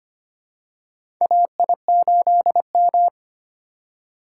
Morse code, heard as a keyed tone, about 25 wpm, 700 Hz. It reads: AI8M
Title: USA Call Signs - 4 Characters - 3x Character Spacing - 25wpm